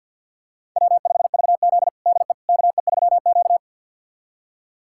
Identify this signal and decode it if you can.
Morse code, heard as a keyed tone, about 52 wpm, 700 Hz. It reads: W5VZ DE KE3Q
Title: USA Callsign DE Callsign - 4 Characters - 50wpm